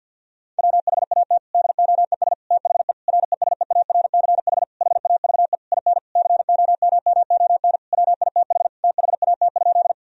Morse code, heard as a keyed tone, about 50 wpm, 700 Hz. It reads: WHAT DOES THE RESEARCH HAVE IN COMMON WITH THAT?
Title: Sentences from Top 1000 Words - Review Part 4 - 50wpm